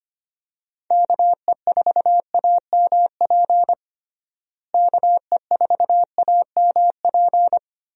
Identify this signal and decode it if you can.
Morse code, heard as a keyed tone, about 25 wpm, 700 Hz. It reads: KE4AMP KE4AMP
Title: USA Call Signs - 6 Characters - Repeated 2x - 25wpm